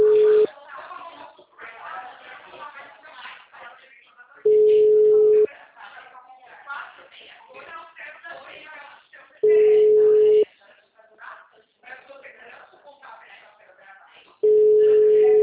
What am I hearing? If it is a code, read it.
Ringback tone (Europe, 425 Hz), with speech.
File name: out- -2033- - - .26540